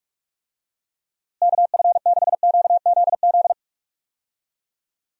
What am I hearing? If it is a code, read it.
Morse code, heard as a keyed tone, about 41 wpm, 700 Hz. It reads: KF6QZZ
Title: USA Call Signs - 6 Characters - Repeated 2x - 45wpm